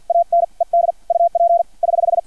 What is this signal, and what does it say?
Morse code, heard as a keyed tone, about 50 wpm, 680 Hz. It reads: AN ED UP <HH>
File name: archived / CRSnet